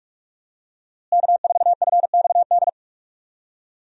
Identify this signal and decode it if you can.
Morse code, heard as a keyed tone, about 46 wpm, 700 Hz. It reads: K4FXD
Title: USA Call Signs - 5 Characters - Repeated 3x - 45wpm